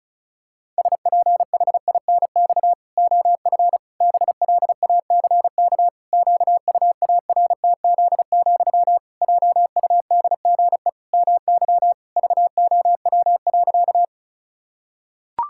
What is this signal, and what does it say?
Morse code, heard as a keyed tone, about 34 wpm, 700 Hz. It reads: SPHINX OF BLACK QUARTZ, JUDGE MY VOW.
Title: Pangram Challenge v2 - 35wpm